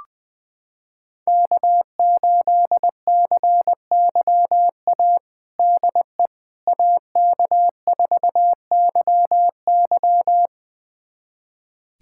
Morse code, heard as a keyed tone, about 21 wpm, 700 Hz. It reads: K8CYA DE AK4YY
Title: USA Callsign DE Callsign - 5 Characters - 20wpm